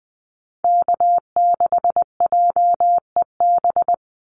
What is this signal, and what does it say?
Morse code, heard as a keyed tone, about 20 wpm, 700 Hz. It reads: K6JEB
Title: Call Signs - 20wpm